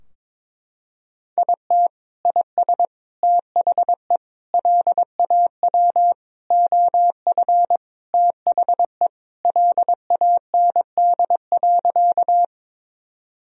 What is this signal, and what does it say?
Morse code, heard as a keyed tone, about 22 wpm, 700 Hz. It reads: IT IS THE LAW OF THE LAND.
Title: Sentences from Top 1000 Words - Review Part 3 - 22wpm